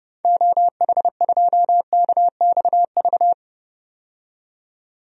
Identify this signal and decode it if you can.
Morse code, heard as a keyed tone, about 30 wpm, 700 Hz. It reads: OH2KXV
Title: DXCC-100 Call Signs - 6 Characters - 30wpm